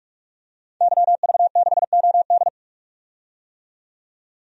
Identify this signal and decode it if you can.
Morse code, heard as a keyed tone, about 42 wpm, 700 Hz. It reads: YV6OD
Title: DXCC-100 Call Signs - 5 Characters - 45wpm